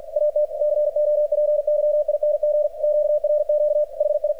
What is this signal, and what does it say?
Morse code, heard as a keyed tone, about 53 wpm, 600 Hz. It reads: AT COPYING QRQ HI
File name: archived / CRSnet